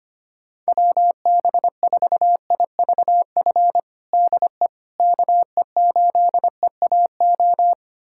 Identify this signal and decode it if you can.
Morse code, heard as a keyed tone, about 25 wpm, 700 Hz. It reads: WB4IVF DE KE8EAO
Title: USA Callsign DE Callsign 25wpm